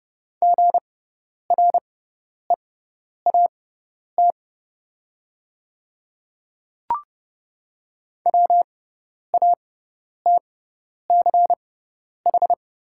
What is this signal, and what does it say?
Morse code, heard as a keyed tone, about 30 wpm, spaced as for 10 wpm, 700 Hz. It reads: GREAT WATCH
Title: 6x Character Spacing - Top 300 Words - 30wpm